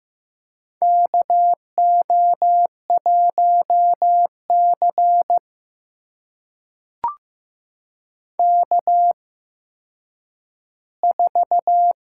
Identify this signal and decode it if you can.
Morse code, heard as a keyed tone, about 15 wpm, spaced as for 9 wpm, 700 Hz. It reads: KO1C K4
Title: USA Call Signs - 4 Characters - 8x Character Spacing - 15wpm